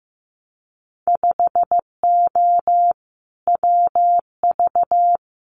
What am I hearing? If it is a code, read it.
Morse code, heard as a keyed tone, about 15 wpm, 700 Hz. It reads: 5O WV
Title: Winter Field Day Exchanges Encore 15wpm